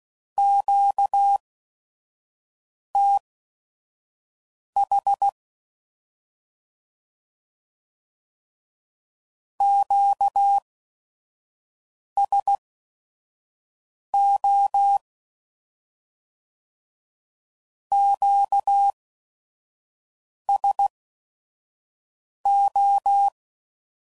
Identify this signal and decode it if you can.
Morse code, heard as a keyed tone, about 16 wpm, spaced as for 5 wpm, 800 Hz. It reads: QTH QSO QSO
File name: CW Code Groups @5WPM / Q Calls / Q Calls Twice